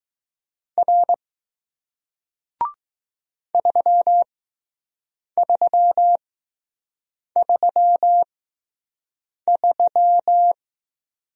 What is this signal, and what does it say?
Morse code, heard as a keyed tone, about 20 wpm, spaced as for 6 wpm, 700 Hz. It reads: R 3333